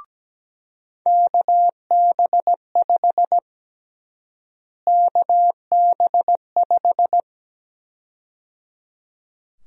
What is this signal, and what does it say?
Morse code, heard as a keyed tone, about 17 wpm, 700 Hz. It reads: KB5 KB5
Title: USA Call Sign Prefixes - Repeated 2x - 17wpm